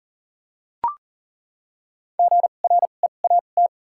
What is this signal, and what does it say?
Morse code, heard as a keyed tone, about 40 wpm, spaced as for 29 wpm, 700 Hz. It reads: GREAT
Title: 2x Character Spacing - Top 200 Words - 40wpm